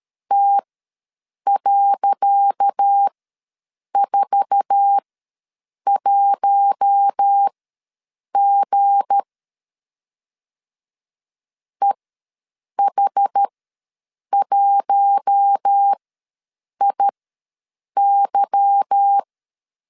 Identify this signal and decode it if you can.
Morse code, heard as a keyed tone, about 13 wpm, spaced as for 7 wpm, 800 Hz. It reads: T.41G EH1IY
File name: LEZIONI / 10 SIGNS / 10C DOT ws 12wpm 3548